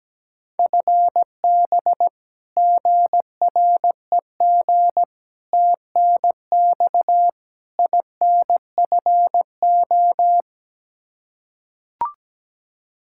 Morse code, heard as a keyed tone, about 17 wpm, 700 Hz. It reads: FB GREG TNX INFO
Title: Practice QSO - Exchange 3 - More Info - 17wpm